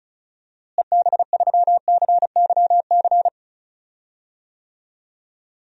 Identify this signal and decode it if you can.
Morse code, heard as a keyed tone, about 34 wpm, 700 Hz. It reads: EB3CYC
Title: DXCC-100 Call Signs - 6 Characters - Repeated 3x - 35wpm